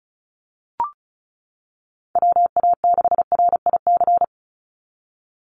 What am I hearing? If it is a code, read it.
Morse code, heard as a keyed tone, about 34 wpm, 700 Hz. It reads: WA6RIC
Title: NA SOTA Chaser Callsigns 35wpm